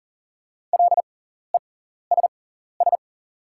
Morse code, heard as a keyed tone, about 40 wpm, spaced as for 14 wpm, 690 Hz. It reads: LESS
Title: 6x Character Spacing - Top 500 Words - 40wpm